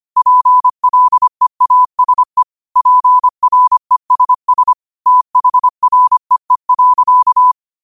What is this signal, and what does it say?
Morse code, heard as a keyed tone, about 25 wpm, 1000 Hz. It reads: PLEASE PRESS THREE.
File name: qotdfiles / 25wpm